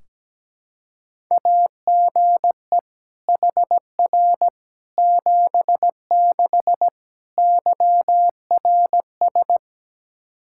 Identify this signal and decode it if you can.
Morse code, heard as a keyed tone, about 17 wpm, 700 Hz. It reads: AGE HR 76 YRS
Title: Practice QSO - Exchange 3 - More Info - 17wpm